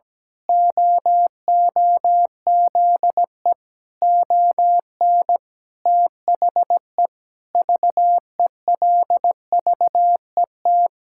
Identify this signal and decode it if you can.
Morse code, heard as a keyed tone, about 17 wpm, 700 Hz. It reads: OOZE ON THE VELVET
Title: Pangram Challenge v2 - 17wpm